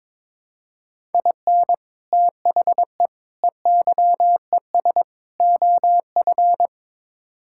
Morse code, heard as a keyed tone, about 22 wpm, 700 Hz. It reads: IN THE EYES OF
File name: practice / 22wpm / Build A Phrase - 5-Word Phrases - 22wpm